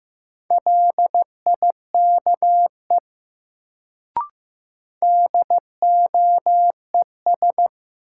Morse code, heard as a keyed tone, about 15 wpm, 700 Hz. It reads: LIKE DOES
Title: Sets of 2 Words - New Word Drill - 15wpm